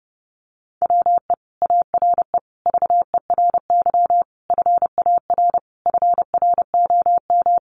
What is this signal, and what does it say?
Morse code, heard as a keyed tone, about 28 wpm, 700 Hz. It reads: WE ARE VERY FAR FROM
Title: Sentences from Top 300 Words 30wpm